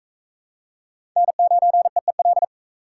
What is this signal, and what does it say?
Morse code, heard as a keyed tone, about 37 wpm, 700 Hz. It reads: N9EEL